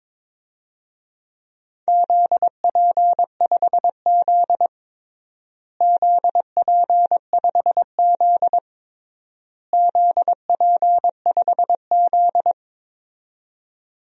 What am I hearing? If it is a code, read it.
Morse code, heard as a keyed tone, about 22 wpm, 700 Hz. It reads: ZP5Z ZP5Z ZP5Z
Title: DXCC-100 Call Signs - 4 Characters - Repeated 3x - 22wpm